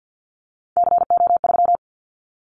Morse code, heard as a keyed tone, about 50 wpm, 690 Hz. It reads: CO3
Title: Callsign Prefixes for Top 100 DXCC - 50wpm